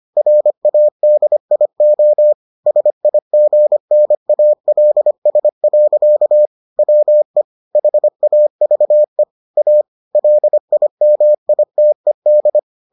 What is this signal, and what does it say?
Morse code, heard as a keyed tone, about 25 wpm, 600 Hz. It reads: RADIO SIGNALS. WE HAVE A LIMITED